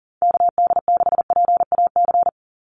Morse code, heard as a keyed tone, about 40 wpm, 700 Hz. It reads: KD6PAC
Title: USA Call Signs 40wpm